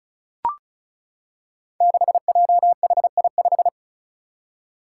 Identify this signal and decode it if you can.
Morse code, heard as a keyed tone, about 34 wpm, 700 Hz. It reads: 6JHI5
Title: Five Challenging Letter-Number Combinations 35wpm